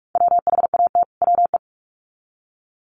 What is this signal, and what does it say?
Morse code, heard as a keyed tone, about 44 wpm, 700 Hz. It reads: WHAT WE
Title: Sets of 2 Words 45wpm